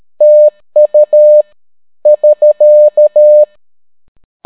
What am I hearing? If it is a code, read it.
Morse code, heard as a keyed tone, about 13 wpm, 600 Hz. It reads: TU <SK>